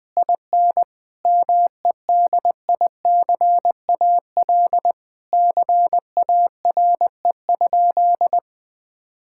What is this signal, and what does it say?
Morse code, heard as a keyed tone, about 20 wpm, 700 Hz. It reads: IN MEDICAL CARE?
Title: Sentences from Top 1000 Words - Review Part 5 - 20wpm